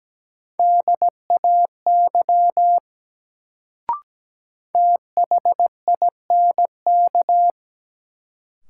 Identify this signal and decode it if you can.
Morse code, heard as a keyed tone, about 17 wpm, 700 Hz. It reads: DAY THINK
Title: Sentences from Top 200 Words - New Word Drill - 17wpm